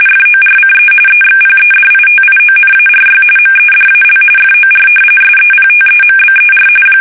RTTY, about 45.45 baud. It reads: FORCE EXPEDITIONARY MEDAL FOR SEPTEMBER
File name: fsk850 / battleshipnj